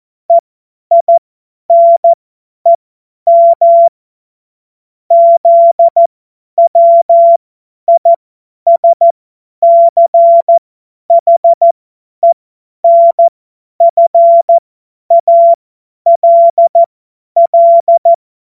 Morse code, heard as a keyed tone, about 14 wpm, spaced as for 10 wpm, 680 Hz. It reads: EINEM ZWISCHENFALL
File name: Morsekurs / Witze / Witz 09. November 2021, 50 F70